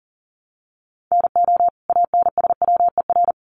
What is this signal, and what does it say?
Morse code, heard as a keyed tone, about 40 wpm, 700 Hz. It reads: NO ANSWER
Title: Sentences from Top 700 Words 40wpm